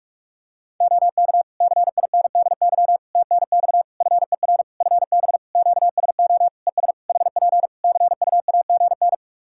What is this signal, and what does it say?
Morse code, heard as a keyed tone, about 46 wpm, 700 Hz. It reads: OK CINDY TNX FER FB QSO ES HP CUAGN
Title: K1USN SST QSO Practice 45wpm